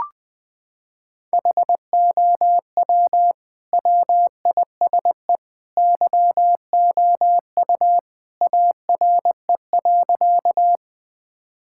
Morse code, heard as a keyed tone, about 21 wpm, 700 Hz. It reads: HOW WISE YOU ARE.
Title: Sentences from Top 1600 Words 20wpm